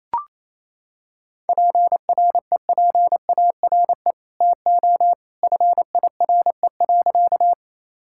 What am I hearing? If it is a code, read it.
Morse code, heard as a keyed tone, about 28 wpm, 700 Hz. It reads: PREPARE TO FIRE.
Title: Sentences from Top 1000 Words - Review Part 6 - 28wpm